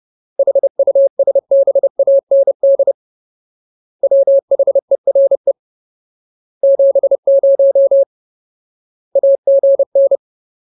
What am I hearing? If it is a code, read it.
Morse code, heard as a keyed tone, about 30 wpm, 550 Hz. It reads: HUSBAND WHERE 70 AGN
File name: Ultimate Word List / 30WPM Ultimate